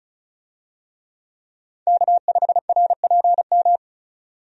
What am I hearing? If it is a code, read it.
Morse code, heard as a keyed tone, about 34 wpm, 700 Hz. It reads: K5RPM
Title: USA Call Signs - 5 Characters - 35wpm